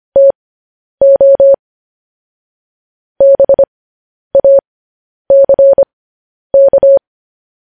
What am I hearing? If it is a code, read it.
Morse code, heard as a keyed tone, about 25 wpm, spaced as for 10 wpm, 550 Hz. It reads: TO BACK
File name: podcast / News headlines-1x-25x10